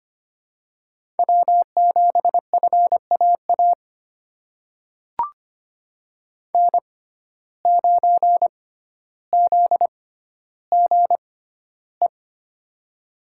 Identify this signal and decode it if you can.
Morse code, heard as a keyed tone, about 25 wpm, spaced as for 13 wpm, 700 Hz. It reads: W7FAA N9ZGE
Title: USA Call Signs - 5 Characters - 6x Character Spacing - 25wpm